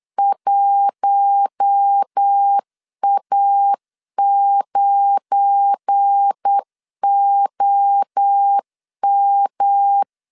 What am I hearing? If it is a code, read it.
Morse code, heard as a keyed tone, about 8 wpm, 800 Hz. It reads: 1A9OM
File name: LEZIONI / 8 YJ45 / 8 YJ 8wpm 3313